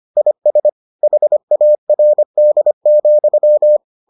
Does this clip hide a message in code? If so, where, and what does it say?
Yes: Morse code from 0.2 to 3.8 s, heard as a keyed tone, about 25 wpm, 600 Hz. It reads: IS HARD,